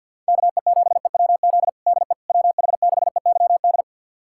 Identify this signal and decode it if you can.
Morse code, heard as a keyed tone, about 51 wpm, 700 Hz. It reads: KE7EWZ DE WH6EYD
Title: USA Callsign DE Callsign 50wpm